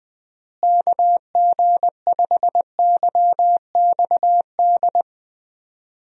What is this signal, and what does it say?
Morse code, heard as a keyed tone, about 20 wpm, 700 Hz. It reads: KG5YXD
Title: USA Call Signs - 6 Characters - Repeated 3x - 20wpm